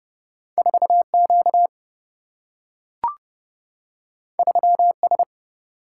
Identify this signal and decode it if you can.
Morse code, heard as a keyed tone, about 30 wpm, 700 Hz. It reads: 4Q 3S
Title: Single-Letter Single-Number 30wpm